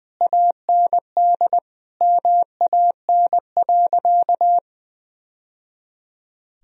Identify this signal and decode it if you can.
Morse code, heard as a keyed tone, about 20 wpm, 700 Hz. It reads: AND MAN.
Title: Sentences from Top 1000 Words - Review Part 1 - 20wpm